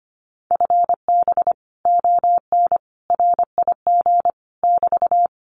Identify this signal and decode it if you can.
Morse code, heard as a keyed tone, about 25 wpm, 700 Hz. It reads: FB ON RIG <BT>
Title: Common QSO Phrases Encore 25wpm